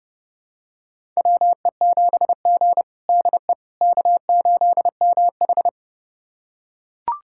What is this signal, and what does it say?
Morse code, heard as a keyed tone, about 30 wpm, 700 Hz. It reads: WE7G DE K8MH
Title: USA Callsign DE Callsign - 4 Characters - 30wpm